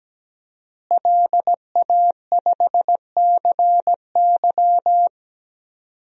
Morse code, heard as a keyed tone, about 17 wpm, 700 Hz. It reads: LA5CY
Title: DXCC-100 Call Signs - 5 Characters - Repeated 3x - 17wpm